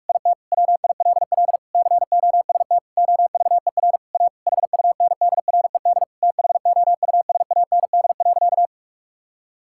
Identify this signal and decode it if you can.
Morse code, heard as a keyed tone, about 41 wpm, 700 Hz. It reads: IT WILL COST OVER A HUNDRED THOUSAND.
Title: Sentences from Top 1000 Words - Review Part 5 - 45wpm